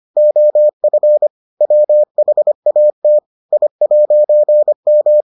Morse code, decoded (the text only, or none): OF WHAT I'M